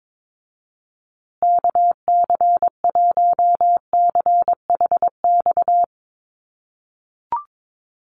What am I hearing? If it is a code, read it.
Morse code, heard as a keyed tone, about 22 wpm, 700 Hz. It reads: KC1CHX
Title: USA Call Signs Repeated 3x 22wpm